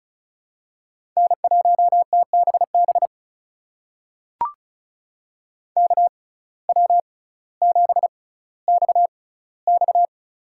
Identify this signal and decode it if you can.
Morse code, heard as a keyed tone, about 36 wpm, spaced as for 18 wpm, 700 Hz. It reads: N1TBB KW7XX
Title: USA Call Signs - 5 Characters - 6x Character Spacing - 35wpm